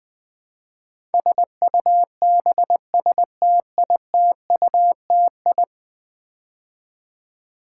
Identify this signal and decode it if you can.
Morse code, heard as a keyed tone, about 20 wpm, 700 Hz. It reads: SUBSTITUTI
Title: Build A Word - 5 Letters At A Time - 12-Letter Words - 20wpm